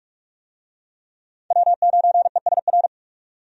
Morse code, heard as a keyed tone, about 47 wpm, 700 Hz. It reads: W9ESR
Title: USA Call Signs - 5 Characters - 45wpm